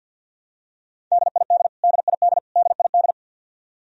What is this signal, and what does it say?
Morse code, heard as a keyed tone, about 53 wpm, 700 Hz. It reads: DID DID DID
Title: Top 100 Words Repeated 50wpm